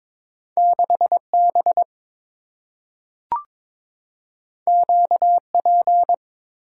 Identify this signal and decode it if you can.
Morse code, heard as a keyed tone, about 22 wpm, 700 Hz. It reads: 6B QP
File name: practice / 22wpm / Two Letter-Number Combinations - 22wpm